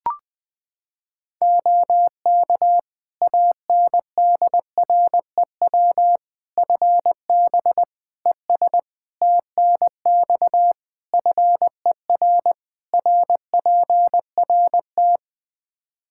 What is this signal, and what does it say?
Morse code, heard as a keyed tone, about 21 wpm, 700 Hz. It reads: OK ANDREW FB ES TNX FER RPRT
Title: Practice QSO - Exchange 2 - Info - 20wpm